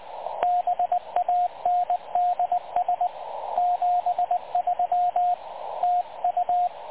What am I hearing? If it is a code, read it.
Morse code, heard as a keyed tone, about 22 wpm, 750 Hz, with radio noise. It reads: BANDS 73 TU
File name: dlazaawansowanych / nagraniazpasma / websdr recording 3565.0kHz